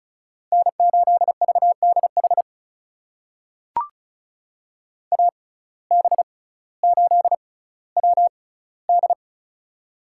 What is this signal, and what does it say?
Morse code, heard as a keyed tone, about 36 wpm, spaced as for 18 wpm, 700 Hz. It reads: N8VDH AB8WD
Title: USA Call Signs - 5 Characters - 6x Character Spacing - 35wpm